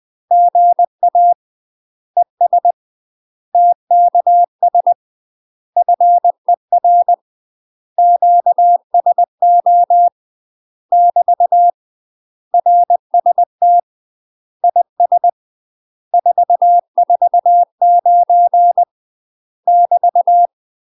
Morse code, heard as a keyed tone, about 20 wpm, 700 Hz. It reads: GA ES TKS FER QSO <BT> RST IS 449 <BT>
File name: UK MASTER QSOs TRAINING / 20WPM / UK MASTER QSOs 0022